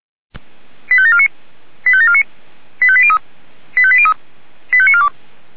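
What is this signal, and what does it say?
Five-tone selective call (ZVEI), heard as quick runs of five tones, tones of about 70 ms. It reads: 85749 85749 85803 85803 85942
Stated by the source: ZVEI five-tone selective call received on SDR.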